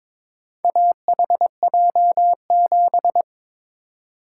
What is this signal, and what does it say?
Morse code, heard as a keyed tone, about 22 wpm, 700 Hz. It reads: AHJ7
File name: practice / 22wpm / Four Letter-Number Combinations - 22wpm